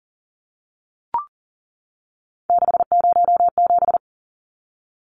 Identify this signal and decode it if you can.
Morse code, heard as a keyed tone, about 40 wpm, 700 Hz. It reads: 607
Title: Numbers - 3-Digit 40wpm